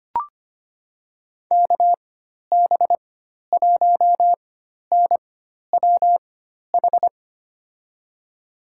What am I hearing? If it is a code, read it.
Morse code, heard as a keyed tone, about 25 wpm, spaced as for 12 wpm, 700 Hz. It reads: KB1NWH
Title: USA Call Signs - 6 Characters - 4x Character Spacing - 25wpm